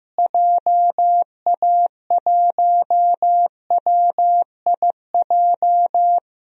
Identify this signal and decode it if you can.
Morse code, heard as a keyed tone, about 15 wpm, 700 Hz. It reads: JA1WIJ
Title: DXCC-100 Call Signs - 6 Characters - Repeated 2x - 15wpm